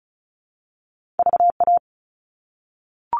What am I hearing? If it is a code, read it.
Morse code, heard as a keyed tone, about 36 wpm, 700 Hz. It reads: VA
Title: All Cut Numbers - 2-Digit 35wpm